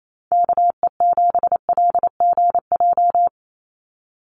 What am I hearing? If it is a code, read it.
Morse code, heard as a keyed tone, about 28 wpm, 700 Hz. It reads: KE7LGJ
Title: USA Call Signs 28wpm